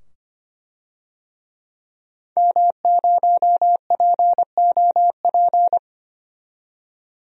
Morse code, heard as a keyed tone, about 25 wpm, 700 Hz. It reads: M0POP
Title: DXCC-100 Call Signs - 5 Characters - 25wpm